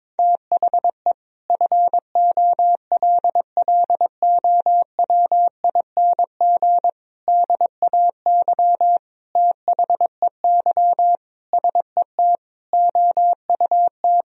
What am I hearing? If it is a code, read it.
Morse code, heard as a keyed tone, about 22 wpm, 700 Hz. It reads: THE FOLLOWING DAY THEY SET OUT
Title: Sentences from Top 1000 Words - Review Part 4 - 22wpm